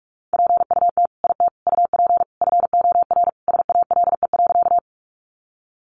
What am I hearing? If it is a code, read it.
Morse code, heard as a keyed tone, about 44 wpm, 700 Hz. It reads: PUT IT UP FOR SALE.
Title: Sentences from Top 1400 Words 45wpm